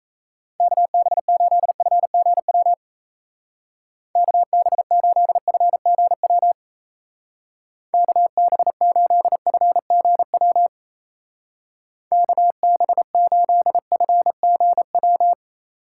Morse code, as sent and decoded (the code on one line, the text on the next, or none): -.- -... ---.. ..-. --. .-- / -.- -... ---.. ..-. --. .-- / -.- -... ---.. ..-. --. .-- / -.- -... ---.. ..-. --. .--
KB8FGW KB8FGW KB8FGW KB8FGW